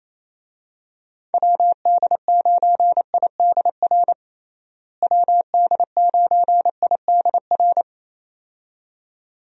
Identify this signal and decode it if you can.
Morse code, heard as a keyed tone, about 28 wpm, 700 Hz. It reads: WD9IDR WD9IDR
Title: USA Call Signs - 6 Characters - Repeated 2x - 28wpm